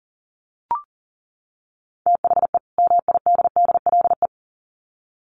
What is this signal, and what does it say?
Morse code, heard as a keyed tone, about 40 wpm, 700 Hz. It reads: THE MIDDLE
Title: Sets of 2 Words 40wpm